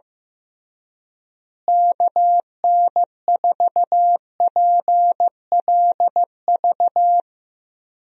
Morse code, heard as a keyed tone, about 16 wpm, 700 Hz. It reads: KN4PLV
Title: USA Call Signs - 6 Characters - Repeated 3x - 15wpm